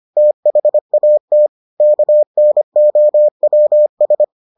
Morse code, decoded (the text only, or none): THAT KNOWS